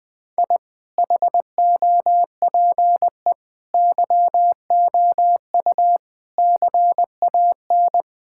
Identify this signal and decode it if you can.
Morse code, heard as a keyed tone, about 20 wpm, 700 Hz. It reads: I HOPE YOU CAN
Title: Sentences with 8-Word Phrases - 20wpm